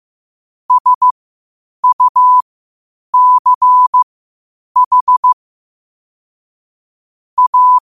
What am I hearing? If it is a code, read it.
Morse code, heard as a keyed tone, about 15 wpm, spaced as for 8 wpm, 1000 Hz. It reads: SUCH A
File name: 8wpm